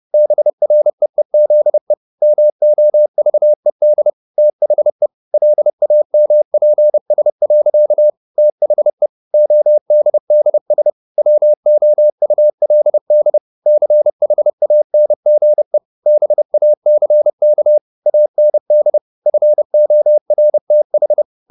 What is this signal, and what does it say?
Morse code, heard as a keyed tone, about 30 wpm, 600 Hz. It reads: BREEZE MOVED THE LAMPS. THE ODDS WOULD CHANGE BACK AND FORTH
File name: the old man and the sea 2 30wpm